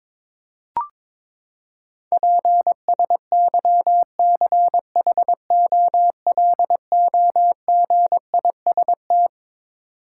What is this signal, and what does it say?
Morse code, heard as a keyed tone, about 22 wpm, 700 Hz. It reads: PSYCHOLOGIST
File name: Twelve Letter Words - 22wpm